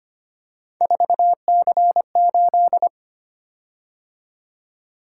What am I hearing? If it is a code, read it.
Morse code, heard as a keyed tone, about 25 wpm, 700 Hz. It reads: 4C8
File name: Three Letter-Number Combinations - 25wpm